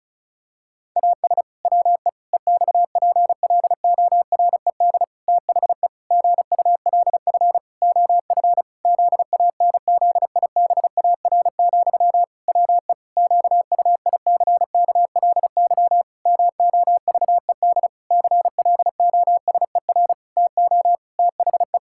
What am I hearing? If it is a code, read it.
Morse code, heard as a keyed tone, about 34 wpm, 700 Hz. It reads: AS WE EXPLORED THE GULF OF ZANZIBAR, WE QUICKLY MOVED CLOSER TO THE
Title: Pangram Challenge v2 - 35wpm